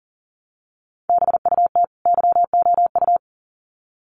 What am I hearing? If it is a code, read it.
Morse code, heard as a keyed tone, about 38 wpm, 700 Hz. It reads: BUT YOU
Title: Sets of 2 Words 40wpm